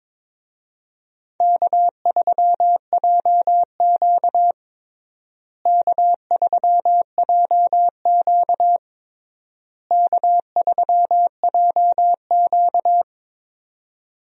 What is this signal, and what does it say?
Morse code, heard as a keyed tone, about 22 wpm, 700 Hz. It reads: K3JQ K3JQ K3JQ
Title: USA Call Signs - 4 Characters - Repeated 3x - 22wpm